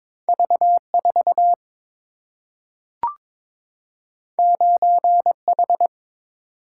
Morse code, heard as a keyed tone, about 22 wpm, 700 Hz. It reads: V 4 9 H
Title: Single-Letter Single-Number 22wpm